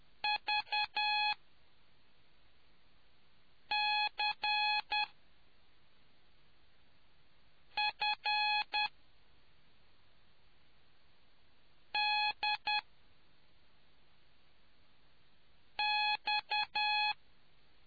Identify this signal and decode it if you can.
Morse code, heard as a keyed tone, about 10 wpm, spaced as for 3 wpm, 2500 Hz. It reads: VCFDX